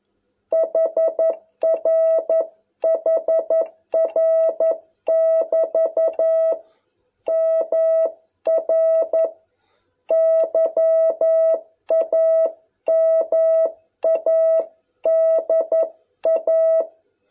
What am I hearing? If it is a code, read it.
Morse code, heard as a keyed tone, about 11 wpm, 660 Hz. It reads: HR HR <BT> MR YAMADA